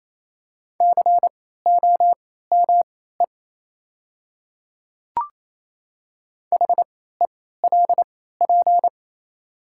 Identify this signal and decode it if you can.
Morse code, heard as a keyed tone, about 28 wpm, spaced as for 16 wpm, 700 Hz. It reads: COME HELP
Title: 3x Character Spacing - Top 300 Words - 28wpm